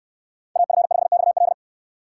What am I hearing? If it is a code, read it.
Morse code, heard as a keyed tone, about 69 wpm, 700 Hz. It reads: AF5XL